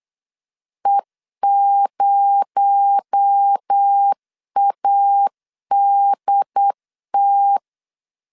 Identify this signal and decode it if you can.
Morse code, heard as a keyed tone, about 8 wpm, 800 Hz. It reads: E0ADT
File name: CW / DET / 1 6 1500C S B 8wpm 5142